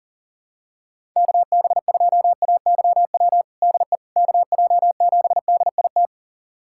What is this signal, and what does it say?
Morse code, heard as a keyed tone, about 40 wpm, 690 Hz. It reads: KB2AYW DE KJ7DIT
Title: USA Callsign DE Callsign - 6 Characters - 40wpm